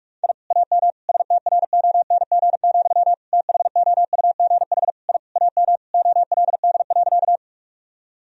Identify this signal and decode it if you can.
Morse code, heard as a keyed tone, about 41 wpm, 700 Hz. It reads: I AM STRONG, THOUGH I AM OLD.
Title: Sentences from Top 1000 Words - Review Part 3 - 45wpm